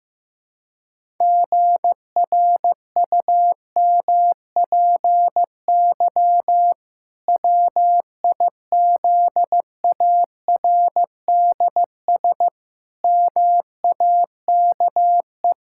Morse code, heard as a keyed tone, about 16 wpm, 700 Hz. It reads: GRUMPY WIZARDS MAKE
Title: Pangram Challenge v2 - 15wpm